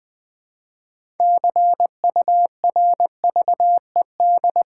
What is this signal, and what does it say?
Morse code, heard as a keyed tone, about 20 wpm, 700 Hz. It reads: CURVED
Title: Build A Word - 3 Letters At A Time - 6-Letter Words - 20wpm